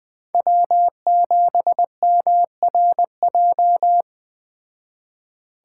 Morse code, heard as a keyed tone, about 20 wpm, 700 Hz. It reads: W7MRJ